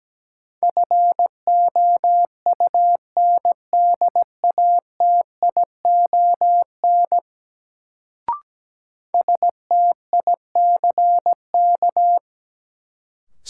Morse code, heard as a keyed tone, about 17 wpm, 700 Hz. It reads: FOUNDATION STICK
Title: Sentences from Top 1200 Words - New Word Drill - 17wpm